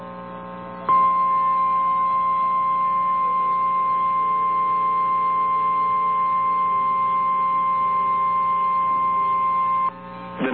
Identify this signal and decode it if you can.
Emergency Alert System (SAME) tones (the 1050 Hz attention tone). It is heard from 0.8 to 10.0 s.
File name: eas / KNCYKBIE / KNCYKBIE